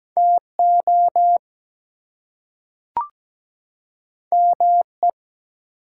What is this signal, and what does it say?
Morse code, heard as a keyed tone, about 17 wpm, 700 Hz. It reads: TO ME
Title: Two Letter Words - 17wpm